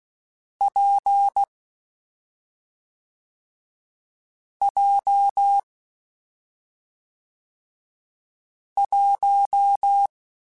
Morse code, heard as a keyed tone, about 16 wpm, 800 Hz. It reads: P J 1
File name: Difficult Combos / Difficult Combos-mp3s / WPJ WPJ1 @5